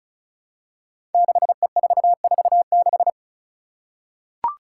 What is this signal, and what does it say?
Morse code, heard as a keyed tone, about 34 wpm, 700 Hz. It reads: 6E446
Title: Five Challenging Letter-Number Combinations 35wpm